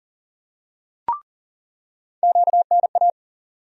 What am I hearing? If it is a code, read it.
Morse code, heard as a keyed tone, about 40 wpm, 700 Hz. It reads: QNA
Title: USA Call Sign Postfixes - 3 Letters 40wpm